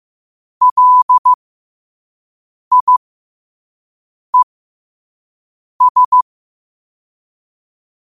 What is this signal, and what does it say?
Morse code, heard as a keyed tone, about 15 wpm, spaced as for 5 wpm, 1000 Hz. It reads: LIES